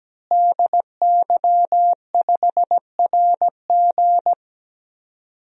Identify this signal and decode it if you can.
Morse code, heard as a keyed tone, about 17 wpm, 700 Hz. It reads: DY5RG
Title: DXCC-100 Call Signs - 5 Characters - Repeated 2x - 17wpm